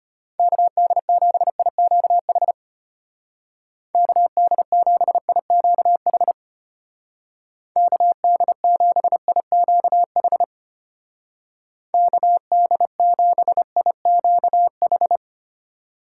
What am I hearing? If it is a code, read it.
Morse code, heard as a keyed tone, about 31 wpm, 700 Hz. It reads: KD7IQH KD7IQH KD7IQH KD7IQH
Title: Speed Racing - USA 2x3 Call Signs 25wpm